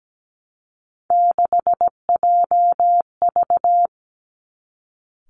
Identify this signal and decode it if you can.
Morse code, heard as a keyed tone, about 17 wpm, 700 Hz. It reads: 6JV
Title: Three Challenging Letter-Number Combinations 17wpm